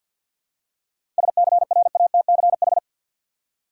Morse code, heard as a keyed tone, about 52 wpm, 700 Hz. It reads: SCRATCH
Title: Seven Letter Words - 50wpm